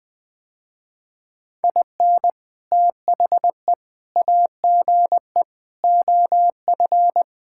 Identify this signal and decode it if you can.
Morse code, heard as a keyed tone, about 20 wpm, 700 Hz. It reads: IN THE AGE OF
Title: Sentences with 3-Word Phrases - Spoken Beginning - 20wpm